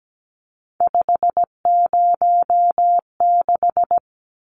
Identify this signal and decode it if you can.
Morse code, heard as a keyed tone, about 17 wpm, 700 Hz. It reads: 506
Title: Numbers - 3-Digit 17wpm